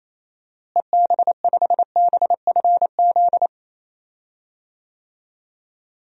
Morse code, heard as a keyed tone, about 27 wpm, 700 Hz. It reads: EB5BFZ
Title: DXCC-100 Call Signs - 6 Characters - Repeated 2x - 28wpm